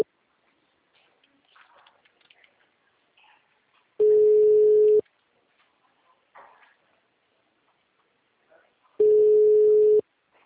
Ringback tone (Europe, 425 Hz).
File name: out- -2013- - - .24892